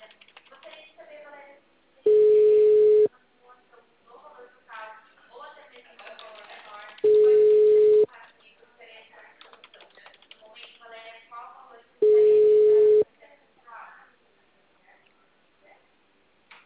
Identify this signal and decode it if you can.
Ringback tone (Europe, 425 Hz), with speech.